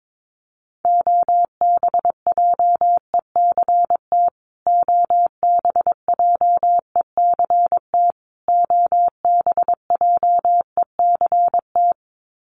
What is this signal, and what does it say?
Morse code, heard as a keyed tone, about 22 wpm, 700 Hz. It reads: OBJECT OBJECT OBJECT
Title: Top 500 Words Repeated 22wpm